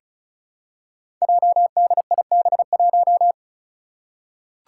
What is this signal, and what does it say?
Morse code, heard as a keyed tone, about 36 wpm, 700 Hz. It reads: JDIB1
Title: Five Challenging Letter-Number Combinations 35wpm